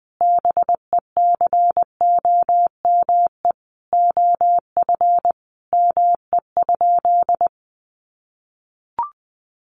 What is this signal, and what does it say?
Morse code, heard as a keyed tone, about 20 wpm, 700 Hz. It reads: BECOME OF ME?
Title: Sentences from Top 600 Words 20wpm